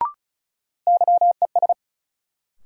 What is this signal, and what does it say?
Morse code, heard as a keyed tone, about 34 wpm, 700 Hz. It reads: YES
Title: Sentences from Top 500 Words - New Word Drill - 35wpm